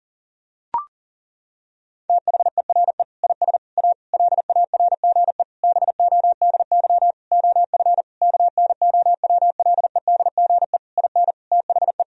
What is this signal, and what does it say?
Morse code, heard as a keyed tone, about 40 wpm, 700 Hz. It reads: THERE IS A LARGE BODY OF KNOWLEDGE IN THE
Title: Sentences from Top 1000 Words - Review Part 4 - 40wpm